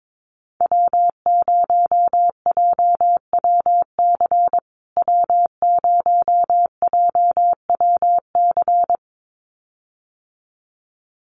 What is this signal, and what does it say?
Morse code, heard as a keyed tone, about 22 wpm, 700 Hz. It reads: W0JWC W0JWC
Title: USA Call Signs Repeated 2x 22wpm